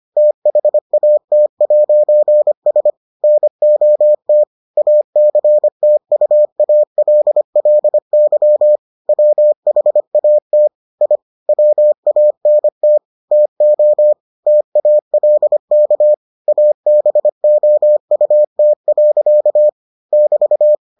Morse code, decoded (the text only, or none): THAT'S NOT ACTUALLY WHAT I WANT TO TALK ABOUT. <BT>